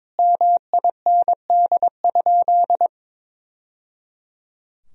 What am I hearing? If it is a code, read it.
Morse code, heard as a keyed tone, about 21 wpm, 700 Hz. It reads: MIND?
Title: Sentences from Top 1000 Words - Review Part 4 - 22wpm